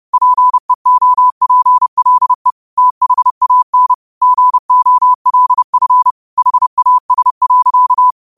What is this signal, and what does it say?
Morse code, heard as a keyed tone, about 30 wpm, 1000 Hz. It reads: PEOPLE THAN GOLF HAS.
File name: qotdfiles / 30wpm